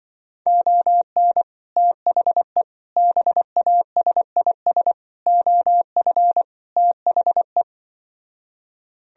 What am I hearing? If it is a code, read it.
Morse code, heard as a keyed tone, about 24 wpm, 700 Hz. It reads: ON THE BASIS OF THE
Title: Speed-Racing - Sets of 5 Words 20wpm